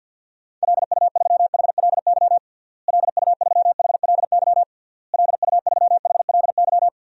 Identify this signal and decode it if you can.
Morse code, heard as a keyed tone, about 52 wpm, 700 Hz. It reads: LU3HLY LU3HLY LU3HLY
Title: DXCC-100 Call Signs - 6 Characters - Repeated 3x - 50wpm